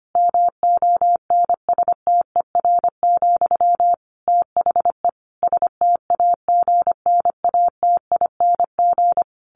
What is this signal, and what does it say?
Morse code, heard as a keyed tone, about 25 wpm, 700 Hz. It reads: MONSTER, THE STAGNATING
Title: A Sound Of Thunder 25wpm